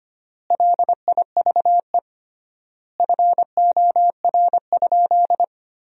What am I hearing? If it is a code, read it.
Morse code, heard as a keyed tone, about 25 wpm, 700 Hz. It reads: LIVE FOR?
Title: 3x Word-Spacing - Sentences from Top 200 Words - 25wpm